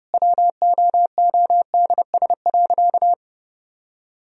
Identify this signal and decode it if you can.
Morse code, heard as a keyed tone, about 30 wpm, 700 Hz. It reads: WOODS.
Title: Sentences from Top 1000 Words - Review Part 2 - 30wpm